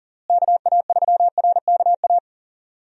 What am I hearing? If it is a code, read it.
Morse code, heard as a keyed tone, about 39 wpm, 700 Hz. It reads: KA3RKA
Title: USA Call Signs - 6 Characters - Repeated 2x - 40wpm